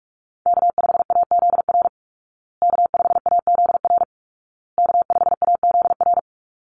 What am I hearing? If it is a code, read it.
Morse code, heard as a keyed tone, about 44 wpm, 700 Hz. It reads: K5AZR K5AZR K5AZR
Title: Call Signs Repeated 2x - 45wpm